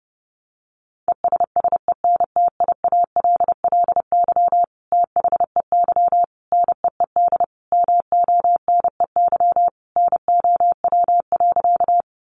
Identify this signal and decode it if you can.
Morse code, heard as a keyed tone, about 30 wpm, 700 Hz. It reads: ESSENTIALLY THEY NEED MONEY NOW.
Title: Sentences from Top 1300 Words 30wpm